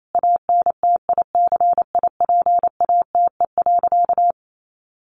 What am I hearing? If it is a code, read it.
Morse code, heard as a keyed tone, about 28 wpm, 700 Hz. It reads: ANTICIPATE.
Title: Sentences from Top 1500 Words 28wpm